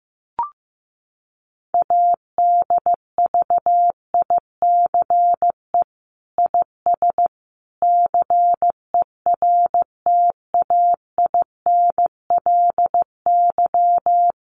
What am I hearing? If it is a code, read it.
Morse code, heard as a keyed tone, about 15 wpm, 700 Hz. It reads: ADVICE IS CERTAINLY
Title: Sentences from Top 1100 Words 15wpm